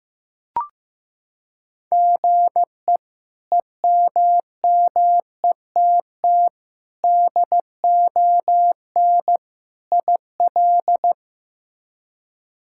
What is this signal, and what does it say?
Morse code, heard as a keyed tone, about 15 wpm, 700 Hz. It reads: GE EMMETT DON IL
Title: K1USN SST QSO Practice - 15wpm